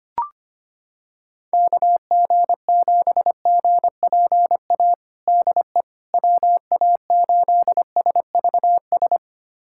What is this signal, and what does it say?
Morse code, heard as a keyed tone, about 25 wpm, 700 Hz. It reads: KG7GPA DE WA8SVS
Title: USA Callsign DE Callsign 25wpm